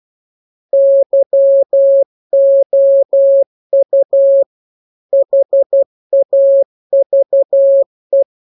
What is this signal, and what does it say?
Morse code, heard as a keyed tone, about 12 wpm, 550 Hz. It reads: YOU HAVE
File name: jokes / joke 5 12 0000